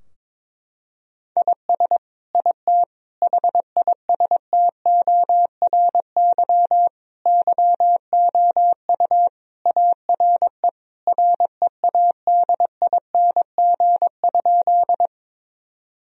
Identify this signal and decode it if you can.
Morse code, heard as a keyed tone, about 22 wpm, 700 Hz. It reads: IS IT HISTORY YOU ARE READING?
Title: Sentences from Top 1000 Words - Review Part 3 - 22wpm